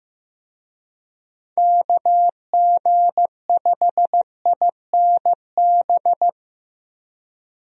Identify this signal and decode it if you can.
Morse code, heard as a keyed tone, about 15 wpm, 700 Hz. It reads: KG5INB
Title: USA Call Signs - 6 Characters - Repeated 2x - 15wpm